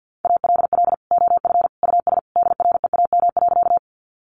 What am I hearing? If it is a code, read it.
Morse code, heard as a keyed tone, about 48 wpm, 700 Hz. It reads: ALL OF US DREAM.
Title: Sentences from Top 1500 Words 50wpm